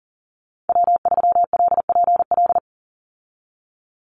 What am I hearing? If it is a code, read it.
Morse code, heard as a keyed tone, about 40 wpm, 700 Hz. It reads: W3LPL
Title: Call Signs Repeated 3x - 40wpm